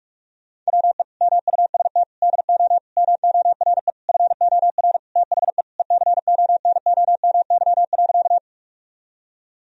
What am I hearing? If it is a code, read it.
Morse code, heard as a keyed tone, about 46 wpm, 700 Hz. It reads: WE MUST DO MORE FOR THE ECONOMY.
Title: Sentences from Top 1000 Words - Review Part 3 - 45wpm